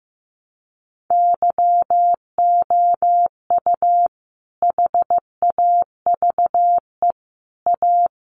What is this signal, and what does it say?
Morse code, heard as a keyed tone, about 15 wpm, 700 Hz. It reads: YOU HAVE A
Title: Sentences from Top 1100 Words 15wpm